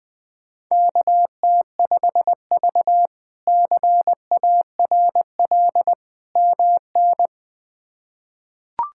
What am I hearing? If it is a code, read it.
Morse code, heard as a keyed tone, about 20 wpm, 700 Hz. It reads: KT5V CARL MN
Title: K1USN SST QSO Practice - 20wpm